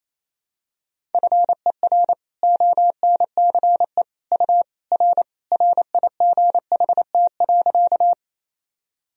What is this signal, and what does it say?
Morse code, heard as a keyed tone, about 28 wpm, 700 Hz. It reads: FER ONCE U R RIGHT.
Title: Sentences from Top 300 Words and Abbreviations 28wpm